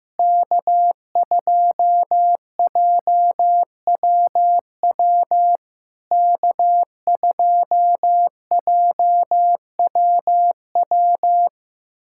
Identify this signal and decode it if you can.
Morse code, heard as a keyed tone, about 15 wpm, 700 Hz. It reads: K2JWW K2JWW
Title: SOTA QSO Practice 15wpm